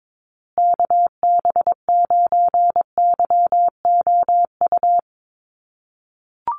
Morse code, heard as a keyed tone, about 22 wpm, 700 Hz. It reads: KB9YOU
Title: USA Call Signs Repeated 2x 22wpm